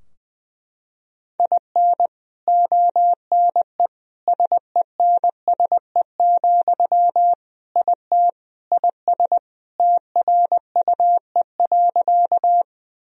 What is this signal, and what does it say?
Morse code, heard as a keyed tone, about 20 wpm, 700 Hz. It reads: IN ONE SENSE, IT IS TRUE.
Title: Sentences from Top 1000 Words - Review Part 4 - 20wpm